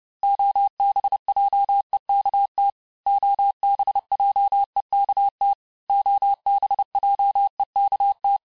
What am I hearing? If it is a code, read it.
Morse code, heard as a keyed tone, about 30 wpm, 800 Hz. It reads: OBJEKT OBJEKT OBJEKT
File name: cours-du-mercredi / audios / cours-854-mots-vitesse-28-mpm